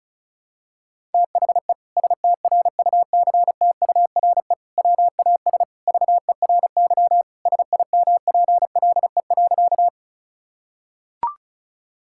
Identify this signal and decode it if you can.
Morse code, heard as a keyed tone, about 34 wpm, 700 Hz. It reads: THE STRUCTURE WAS VERY SIMPLE.
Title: Sentences from Top 1000 Words - Review Part 6 - 35wpm